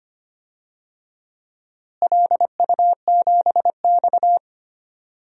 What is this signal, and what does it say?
Morse code, heard as a keyed tone, about 25 wpm, 700 Hz. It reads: LU7X
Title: DXCC-100 Call Signs - 4 Characters - Repeated 3x - 25wpm